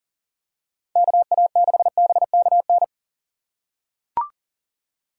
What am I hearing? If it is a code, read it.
Morse code, heard as a keyed tone, about 38 wpm, 700 Hz. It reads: KA6BKN
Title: USA Call Signs - 6 Characters - 4x Character Spacing - 40wpm